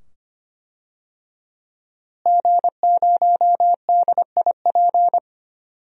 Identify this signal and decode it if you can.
Morse code, heard as a keyed tone, about 25 wpm, 700 Hz. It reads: G0DIP
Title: DXCC-100 Call Signs - 5 Characters - Repeated 2x - 25wpm